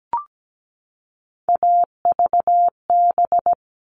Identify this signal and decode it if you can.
Morse code, heard as a keyed tone, about 17 wpm, 700 Hz. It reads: AVB
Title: All Cut Numbers - 3-Digit 17wpm